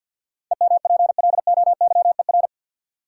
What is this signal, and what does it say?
Morse code, heard as a keyed tone, about 52 wpm, 700 Hz. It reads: EMPLOYER
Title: Sentences from Top 1600 Words - New Word Drill Alt - 50wpm